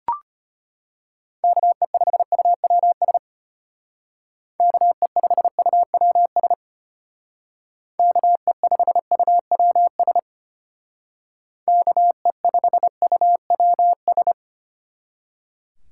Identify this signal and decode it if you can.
Morse code, heard as a keyed tone, about 33 wpm, 700 Hz. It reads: KE5UWS KE5UWS KE5UWS KE5UWS